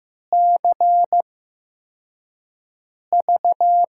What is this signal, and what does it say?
Morse code, heard as a keyed tone, about 15 wpm, spaced as for 4 wpm, 700 Hz. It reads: CV
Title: USA Call Signs - 6 Characters - 8x Character Spacing - 15wpm